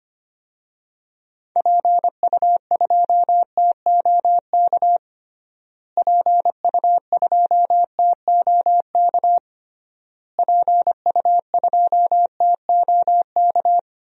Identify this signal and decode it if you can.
Morse code, heard as a keyed tone, about 25 wpm, 700 Hz. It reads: PU2TOK PU2TOK PU2TOK
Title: DXCC-100 Call Signs - 6 Characters - Repeated 3x - 25wpm